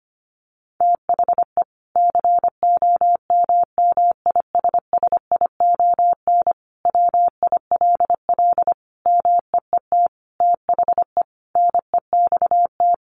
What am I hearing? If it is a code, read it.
Morse code, heard as a keyed tone, about 25 wpm, 700 Hz. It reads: THE COMMISSION WILL MEET THE NEXT
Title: Sentences from Top 1400 Words 25wpm